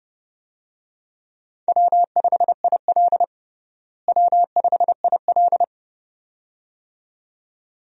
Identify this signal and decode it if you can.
Morse code, heard as a keyed tone, about 32 wpm, 700 Hz. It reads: W5IL W5IL
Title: USA Call Signs - 4 Characters - Repeated 2x - 30wpm